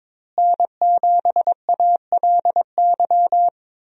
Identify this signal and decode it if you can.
Morse code, heard as a keyed tone, about 22 wpm, 700 Hz. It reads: N7ALY
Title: USA Call Signs - 5 Characters - Repeated 2x - 22wpm